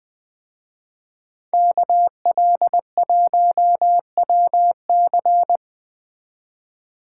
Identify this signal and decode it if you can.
Morse code, heard as a keyed tone, about 21 wpm, 700 Hz. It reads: KL1WC
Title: Speed Racing - USA 2x2 Call Signs 17wpm